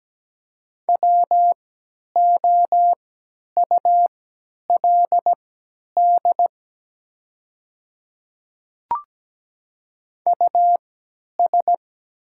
Morse code, heard as a keyed tone, about 17 wpm, spaced as for 10 wpm, 700 Hz. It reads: WOULD US
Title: 3x Character Spacing - Top 300 Words - 17wpm